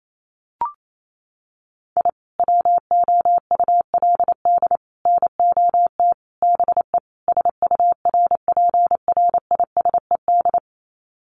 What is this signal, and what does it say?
Morse code, heard as a keyed tone, about 28 wpm, 700 Hz. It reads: I WOULD NOT BE SURPRISED
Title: Sets of 5 Words Encore2 28wpm